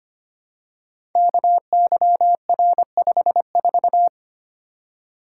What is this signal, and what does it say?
Morse code, heard as a keyed tone, about 25 wpm, 700 Hz. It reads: KYR54
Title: Code Groups - 5 Letter:Numbers - 25wpm